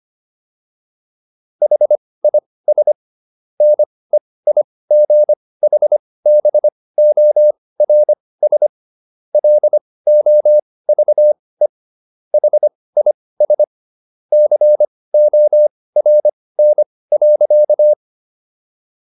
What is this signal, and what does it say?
Morse code, heard as a keyed tone, about 25 wpm, spaced as for 18 wpm, 600 Hz. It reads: HIS NEIGHBORS LOVE HIS CORN.